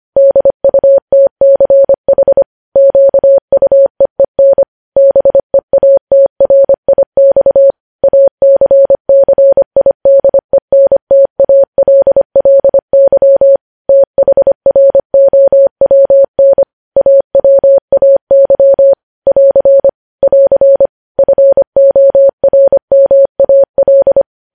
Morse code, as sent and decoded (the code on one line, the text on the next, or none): -.. ..- - -.-. .... / --.- ..- . -. / -... .- - .-. .. -..- / .- -.-. -.-. .. -.. -. - .- .-.. .-.. -.-- / - .... .-. --- .-- -. / .- .-- .- -.-- / .-.-. / .-.-. / ..-. --- .-. -- .- .-..
DUTCH QUEEN BEATRIX ACCIDENTALLY THROWN AWAY <AR> <AR> FORMAL